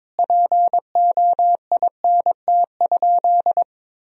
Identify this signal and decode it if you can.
Morse code, heard as a keyed tone, about 22 wpm, 700 Hz. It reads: POINT?
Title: Sentences from Top 1000 Words - Review Part 1 - 22wpm